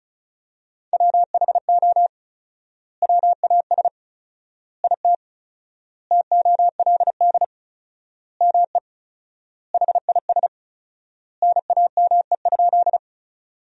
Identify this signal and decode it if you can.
Morse code, heard as a keyed tone, about 36 wpm, 700 Hz. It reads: WHO WAS IT TOLD ME HIS NAME?
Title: 4x Word-Spacing - Sentences from Top 400 Words - 35wpm